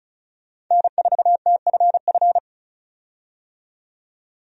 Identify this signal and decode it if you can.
Morse code, heard as a keyed tone, about 36 wpm, 700 Hz. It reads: N4TFF
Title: USA Call Signs - 5 Characters - 35wpm